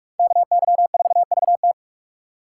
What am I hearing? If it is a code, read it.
Morse code, heard as a keyed tone, about 58 wpm, 700 Hz. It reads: KY4VT